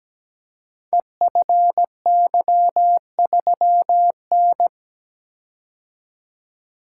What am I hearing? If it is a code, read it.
Morse code, heard as a keyed tone, about 17 wpm, 700 Hz. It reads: EFY3N
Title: Code Groups - 5 Letter:Numbers - 17wpm